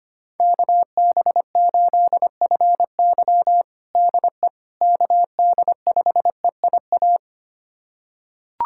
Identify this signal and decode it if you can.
Morse code, heard as a keyed tone, about 25 wpm, 700 Hz. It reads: KB8FY DE KD5EIA
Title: USA Callsign DE Callsign 25wpm